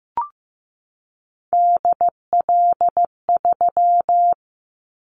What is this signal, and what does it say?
Morse code, heard as a keyed tone, about 15 wpm, 700 Hz. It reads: DL3
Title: Callsign Prefixes for Top 50 DXCC - 15wpm